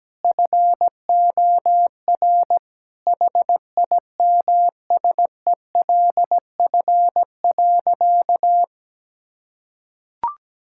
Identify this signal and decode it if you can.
Morse code, heard as a keyed tone, about 17 wpm, 700 Hz. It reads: FOR HIMSELF.
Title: Sentences from Top 1600 Words 17wpm